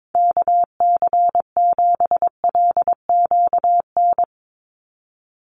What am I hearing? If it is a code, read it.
Morse code, heard as a keyed tone, about 22 wpm, 700 Hz. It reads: KC7LQN
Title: USA Call Signs 22wpm